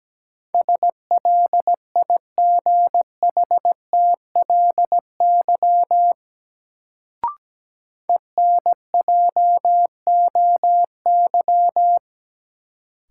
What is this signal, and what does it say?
Morse code, heard as a keyed tone, about 17 wpm, 700 Hz. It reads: SLIGHTLY ENJOY
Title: Sentences from Top 1300 Words - New Word Drill Alt - 17wpm